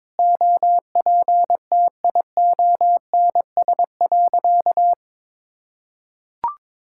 Morse code, heard as a keyed tone, about 22 wpm, 700 Hz. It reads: OPTIONS.
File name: practice / 22wpm / Sentences with 3-Word Phrases - 22wpm